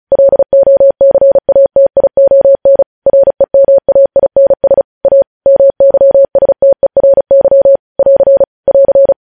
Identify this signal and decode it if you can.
Morse code, heard as a keyed tone, about 36 wpm, 550 Hz. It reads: LOCATION REMAINS A MYSTERY <AR> <AR>
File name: podcast / News headlines-1x-35x35